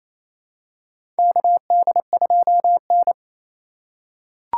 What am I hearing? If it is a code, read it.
Morse code, heard as a keyed tone, about 27 wpm, 700 Hz. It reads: KD2N
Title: USA Call Signs - 4 Characters - 2x Character Spacing - 28wpm